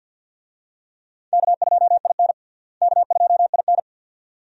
Morse code, heard as a keyed tone, about 50 wpm, 700 Hz. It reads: K2IN K2IN
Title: USA Call Signs - 4 Characters - Repeated 2x - 50wpm